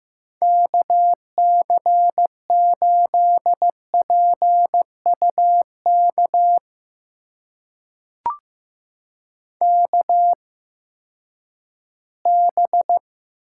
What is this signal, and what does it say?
Morse code, heard as a keyed tone, about 15 wpm, spaced as for 10 wpm, 700 Hz. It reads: KC8PUK KB
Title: USA Call Signs - 6 Characters - 8x Character Spacing - 15wpm